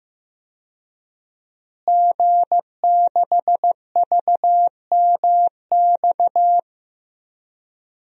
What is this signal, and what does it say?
Morse code, heard as a keyed tone, about 15 wpm, 700 Hz. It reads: G6VMX
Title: DXCC-100 Call Signs - 5 Characters - Repeated 2x - 15wpm